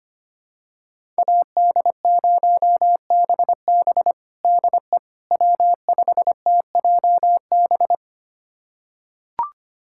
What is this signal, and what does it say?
Morse code, heard as a keyed tone, about 25 wpm, 700 Hz. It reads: AD0BB DE W5TJB
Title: USA Callsign DE Callsign - 5 Characters - 25wpm